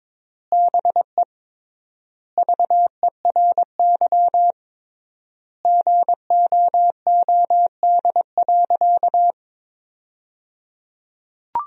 Morse code, heard as a keyed tone, about 22 wpm, 700 Hz. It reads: BE VERY GOOD.
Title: 3x Word-Spacing - Sentences from Top 200 Words - 22wpm